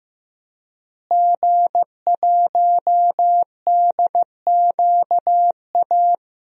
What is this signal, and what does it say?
Morse code, heard as a keyed tone, about 15 wpm, 700 Hz. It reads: G1DQA
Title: DXCC-100 Call Signs - 5 Characters - Repeated 2x - 15wpm